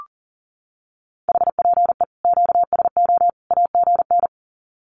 Morse code, heard as a keyed tone, about 40 wpm, 700 Hz. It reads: HPE QSO AGN
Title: Common QSO Phrases 40wpm